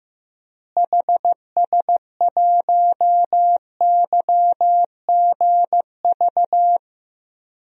Morse code, heard as a keyed tone, about 15 wpm, 700 Hz. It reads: HS1YGV
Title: DXCC-100 Call Signs - 6 Characters - Repeated 2x - 15wpm